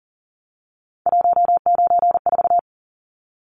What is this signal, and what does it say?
Morse code, heard as a keyed tone, about 40 wpm, 700 Hz. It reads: 194
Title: Numbers - 3-Digit 40wpm